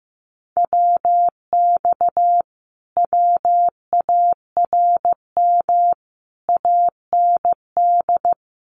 Morse code, heard as a keyed tone, about 15 wpm, 700 Hz. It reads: WX WARM AND
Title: Common QSO Phrases Encore 15wpm